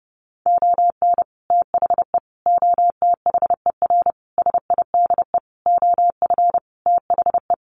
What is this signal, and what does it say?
Morse code, heard as a keyed tone, about 28 wpm, 700 Hz. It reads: ON THE OTHER SIDE OF THE
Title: Sentences from Top 1100 Words 30wpm